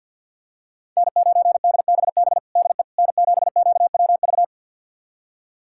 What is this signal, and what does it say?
Morse code, heard as a keyed tone, about 48 wpm, 700 Hz. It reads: N9DBB DE N7QWV
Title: USA Callsign DE Callsign - 5 Characters - 50wpm